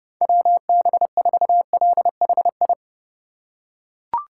Morse code, heard as a keyed tone, about 31 wpm, 700 Hz. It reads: WB4LHI
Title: USA Call Signs - 6 Characters - 6x Character Spacing - 30wpm